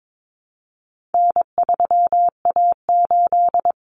Morse code, heard as a keyed tone, about 22 wpm, 700 Hz. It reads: N3A8
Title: Common Cut Numbers - 4-Digit 22wpm